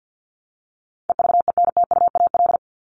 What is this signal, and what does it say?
Morse code, heard as a keyed tone, about 50 wpm, 700 Hz. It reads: EVENTUAL
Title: LY Suffix - Top 100 Words with Base Words 50wpm